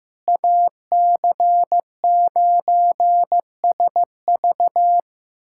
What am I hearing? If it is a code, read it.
Morse code, heard as a keyed tone, about 15 wpm, 700 Hz. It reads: AC9SV
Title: USA Callsign DE Callsign - 5 Characters - 15wpm